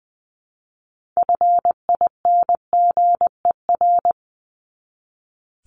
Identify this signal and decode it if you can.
Morse code, heard as a keyed tone, about 20 wpm, 700 Hz. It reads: FINGER
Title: Top 1501-1750 Words 20wpm